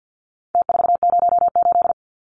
Morse code, heard as a keyed tone, about 48 wpm, 700 Hz. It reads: T408
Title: Numbers - Contest Style - 4-Digit 50wpm